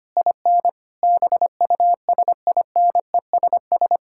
Morse code, heard as a keyed tone, about 25 wpm, 700 Hz. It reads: IN BUSINESS
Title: Sentences from Top 1000 Words - Review Part 6 - 25wpm